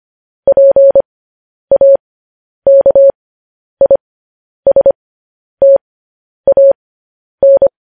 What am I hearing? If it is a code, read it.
Morse code, heard as a keyed tone, about 25 wpm, spaced as for 10 wpm, 550 Hz. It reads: PAKISTAN
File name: podcast / News headlines-1x-25x10